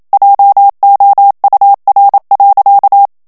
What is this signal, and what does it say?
Morse code, heard as a keyed tone, about 27 wpm, 800 Hz. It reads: JOUR.
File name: CW / FICHIERS-MP3 / FRANCAIS TEXTE CLAIR 1 VITESSE 26 MOTS MINUTES